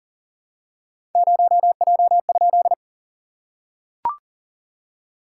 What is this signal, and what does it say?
Morse code, heard as a keyed tone, about 37 wpm, 690 Hz. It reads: 0J?
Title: SOTA QSO Practice 40wpm